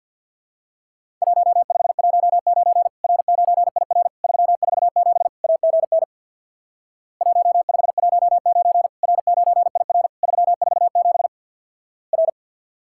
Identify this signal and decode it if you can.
Morse code, heard as a keyed tone, about 50 wpm, 700 Hz. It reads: 1519 R9IR 347 AGN 1519 R9IR 347 R
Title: Worked All Europe DX Contest - QTC Exchange with Repeats - 50wpm